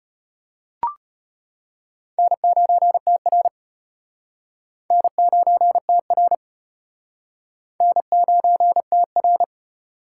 Morse code, heard as a keyed tone, about 34 wpm, 700 Hz. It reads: N9TR N9TR N9TR